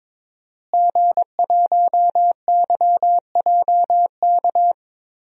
Morse code, heard as a keyed tone, about 22 wpm, 700 Hz. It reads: G1YJK
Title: DXCC-100 Call Signs - 5 Characters - Repeated 2x - 22wpm